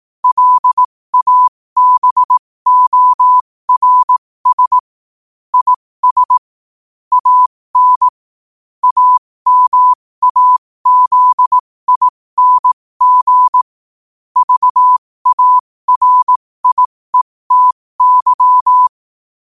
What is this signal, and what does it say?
Morse code, heard as a keyed tone, about 18 wpm, spaced as for 15 wpm, 1000 Hz. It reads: LABORS IS AN AMAZING VARIETY
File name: qotdfiles / 15wpm